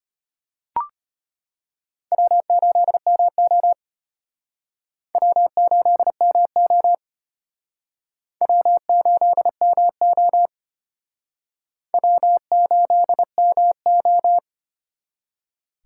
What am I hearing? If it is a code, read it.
Morse code, heard as a keyed tone, about 34 wpm, 700 Hz. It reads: W8MO W8MO W8MO W8MO